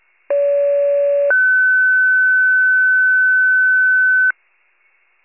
Two-tone page, heard as a one-second tone, then a three-second tone, about 570 then 1500 Hz. It is heard from 0.3 to 4.3 s.